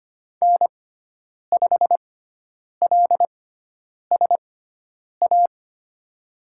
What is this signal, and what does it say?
Morse code, heard as a keyed tone, about 25 wpm, spaced as for 9 wpm, 700 Hz. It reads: N5LSA
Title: USA Call Signs - 5 Characters - 6x Character Spacing - 25wpm